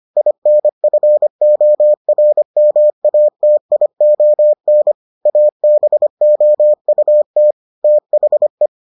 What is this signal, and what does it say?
Morse code, heard as a keyed tone, about 25 wpm, 600 Hz. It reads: INFORMATION ABOUT THE